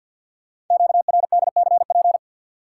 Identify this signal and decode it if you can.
Morse code, heard as a keyed tone, about 47 wpm, 700 Hz. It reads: QRDCP
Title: Code Groups - 5 Letter:Numbers - 50wpm